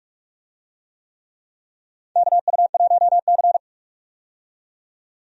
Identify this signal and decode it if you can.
Morse code, heard as a keyed tone, about 46 wpm, 700 Hz. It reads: KU1C
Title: USA Call Signs - 4 Characters - 45wpm